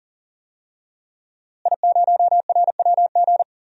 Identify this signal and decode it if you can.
Morse code, heard as a keyed tone, about 37 wpm, 700 Hz. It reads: I0RWG
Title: DXCC-100 Call Signs - 5 Characters - Repeated 2x - 40wpm